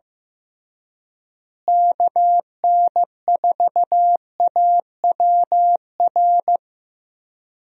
Morse code, heard as a keyed tone, about 15 wpm, 700 Hz. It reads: KN4AWR
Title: USA Call Signs - 6 Characters - Repeated 2x - 15wpm